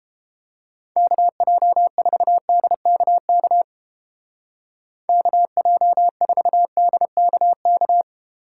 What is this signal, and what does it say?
Morse code, heard as a keyed tone, about 29 wpm, 700 Hz. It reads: KJ4DKK KJ4DKK